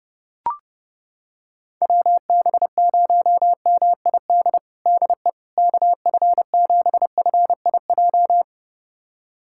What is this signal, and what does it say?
Morse code, heard as a keyed tone, about 29 wpm, 700 Hz. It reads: WB0MID DE KF7FIJ
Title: USA Callsign DE Callsign - 6 Characters - 30wpm